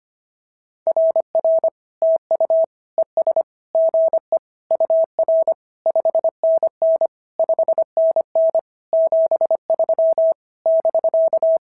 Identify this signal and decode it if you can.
Morse code, heard as a keyed tone, about 24 wpm, 650 Hz. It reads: RR TU ES GE UR 5NN 5NN 73 <BK>
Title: SOTA QSO Practice 25wpm